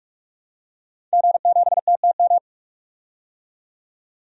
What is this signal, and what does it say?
Morse code, heard as a keyed tone, about 46 wpm, 700 Hz. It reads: G7TTM
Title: DXCC-100 Call Signs - 5 Characters - 45wpm